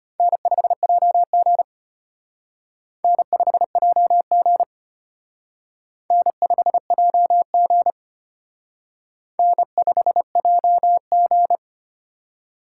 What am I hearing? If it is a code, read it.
Morse code, heard as a keyed tone, about 31 wpm, 700 Hz. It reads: N5JG N5JG N5JG N5JG
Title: Speed Racing - USA 1x2 Call Signs 25wpm